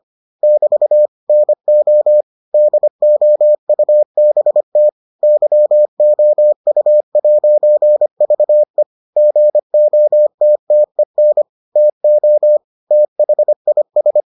Morse code, heard as a keyed tone, about 25 wpm, 600 Hz. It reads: <BT> NO DOUBT YOU'VE GOTTEN TO THIS